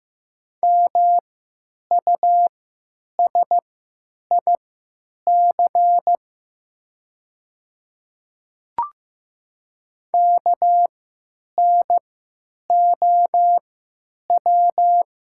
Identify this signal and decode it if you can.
Morse code, heard as a keyed tone, about 15 wpm, spaced as for 9 wpm, 700 Hz. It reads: MUSIC KNOW
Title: 3x Character Spacing - Top 300 Words - 15wpm